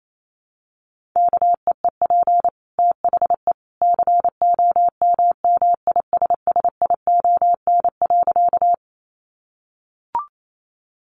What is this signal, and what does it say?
Morse code, heard as a keyed tone, about 28 wpm, 700 Hz. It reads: KEEP THE COMMISSION.
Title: Sentences from Top 1400 Words 28wpm